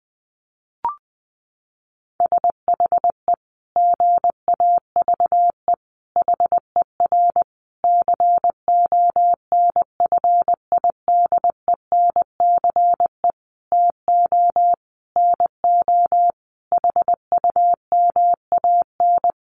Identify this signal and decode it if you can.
Morse code, heard as a keyed tone, about 20 wpm, 700 Hz. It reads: SHE GAVE HER CONFIDENCE TO NO HUMAN
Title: Sentences from Top 1500 Words 20wpm